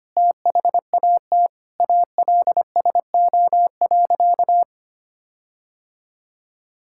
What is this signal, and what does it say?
Morse code, heard as a keyed tone, about 25 wpm, 700 Hz. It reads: THAT ALSO.
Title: Sentences from Top 1600 Words 25wpm